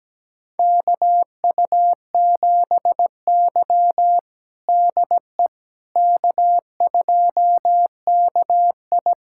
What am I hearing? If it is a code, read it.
Morse code, heard as a keyed tone, about 17 wpm, 700 Hz. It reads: KU7Y DE K2KI
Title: USA Callsign DE Callsign - 4 Characters - 17wpm